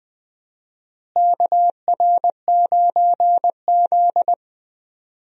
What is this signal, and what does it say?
Morse code, heard as a keyed tone, about 20 wpm, 700 Hz. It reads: KR9Z
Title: USA Call Signs - 4 Characters - Repeated 3x - 20wpm